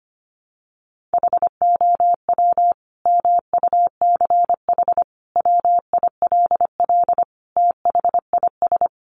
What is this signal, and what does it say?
Morse code, heard as a keyed tone, about 25 wpm, 700 Hz. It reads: HOW MUCH WILL THIS
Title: Sentences from Top 1400 Words 25wpm